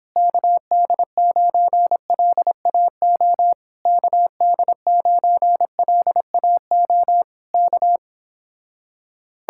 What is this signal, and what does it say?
Morse code, heard as a keyed tone, about 26 wpm, 700 Hz. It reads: KD9LAO KD9LAO K
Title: Speed-Racing Common QSO Phrases - 22wpm